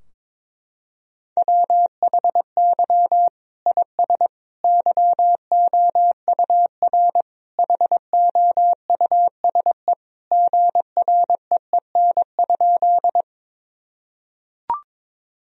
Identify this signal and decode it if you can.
Morse code, heard as a keyed tone, about 22 wpm, 700 Hz. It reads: WHY IS YOUR HOUSE GREEN?
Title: Sentences from Top 1000 Words - Review Part 2 - 22wpm